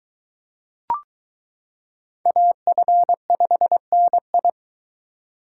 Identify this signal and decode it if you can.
Morse code, heard as a keyed tone, about 23 wpm, 700 Hz. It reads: AF5NI